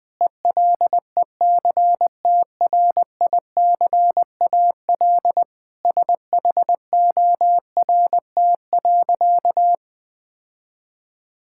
Morse code, heard as a keyed tone, about 20 wpm, 700 Hz. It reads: ELECTRICAL SHORT.
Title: Sentences from Top 1000 Words - Review Part 6 - 20wpm